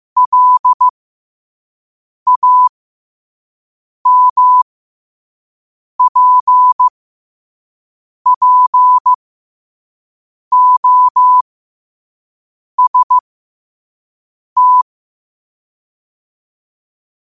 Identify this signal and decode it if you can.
Morse code, heard as a keyed tone, about 15 wpm, spaced as for 5 wpm, 1000 Hz. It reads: LAMPPOST
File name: qotdfiles / 5wpm